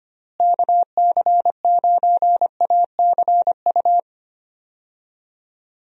Morse code, heard as a keyed tone, about 25 wpm, 700 Hz. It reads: KC9ACU
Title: USA Call Signs - 6 Characters - Repeated 3x - 25wpm